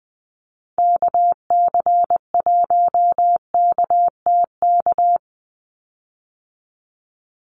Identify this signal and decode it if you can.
Morse code, heard as a keyed tone, about 20 wpm, 700 Hz. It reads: KC1KTK
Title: USA Call Signs 20wpm